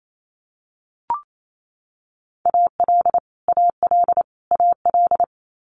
Morse code, heard as a keyed tone, about 28 wpm, 700 Hz. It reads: AL AL AL
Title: Multi-letter Phonemes Repeated 3x 28wpm